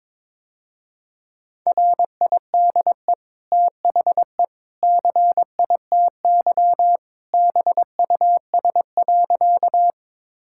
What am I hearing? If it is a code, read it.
Morse code, heard as a keyed tone, about 22 wpm, 700 Hz. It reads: RIDE THE CITY BUS.
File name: practice / 22wpm / Easy Sentences with Hints - 22wpm